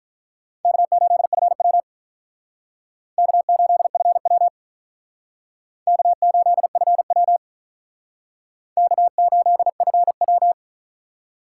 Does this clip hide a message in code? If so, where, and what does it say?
Yes: Morse code from 0.6 to 10.5 s, heard as a keyed tone, about 49 wpm, 700 Hz. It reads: K8FW K8FW K8FW K8FW